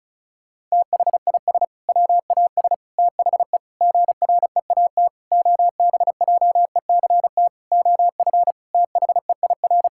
Morse code, heard as a keyed tone, about 36 wpm, 700 Hz. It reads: THIS WAS THE GREAT OBJECT OF THEIR
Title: Sentences from Top 1000 Words - Review Part 5 - 35wpm